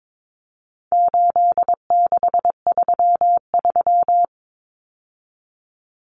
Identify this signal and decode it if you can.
Morse code, heard as a keyed tone, about 22 wpm, 700 Hz. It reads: 8633
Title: Numbers - 4-Digit 22wpm